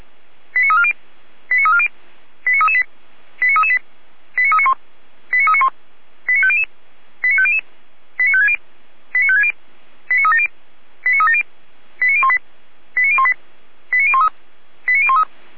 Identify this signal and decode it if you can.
Five-tone selective call (ZVEI), heard as quick runs of five tones, tones of about 70 ms. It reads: 89340 89340 89338 89338 89391 89391 89500 89500 89560 89560 89370 89370 89018 89018 89013 89013